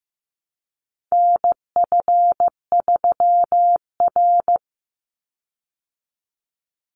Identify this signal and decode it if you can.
Morse code, heard as a keyed tone, about 15 wpm, 700 Hz. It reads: NF3R
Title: Call Signs Repeated 2x - 15wpm